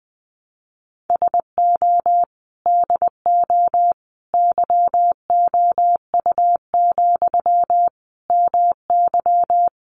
Morse code, heard as a keyed tone, about 20 wpm, 700 Hz. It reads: SO DO YOU, MY
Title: Sentences from Top 400 Words 20wpm